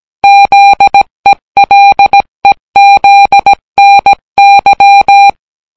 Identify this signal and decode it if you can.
Morse code, heard as a keyed tone, about 17 wpm, 800 Hz. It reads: ZELEZNY